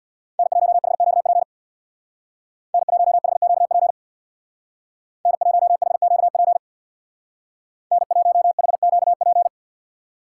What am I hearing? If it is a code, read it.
Morse code, heard as a keyed tone, about 64 wpm, 700 Hz. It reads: N1HQP N1HQP N1HQP N1HQP